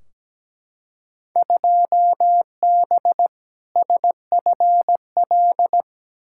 Morse code, heard as a keyed tone, about 17 wpm, 700 Hz. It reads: 2B SFL
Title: Field Day Exchanges Encore - 17wpm